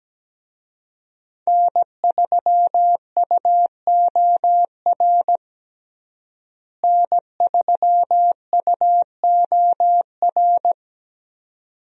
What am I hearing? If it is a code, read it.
Morse code, heard as a keyed tone, about 17 wpm, 700 Hz. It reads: N3UOR N3UOR
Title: USA Call Signs - 5 Characters - Repeated 2x - 17wpm